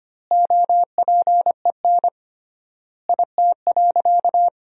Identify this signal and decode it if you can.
Morse code, heard as a keyed tone, about 25 wpm, 700 Hz. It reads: OPEN IT.
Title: 3x Word-Spacing - Sentences from Top 300 Words - 25wpm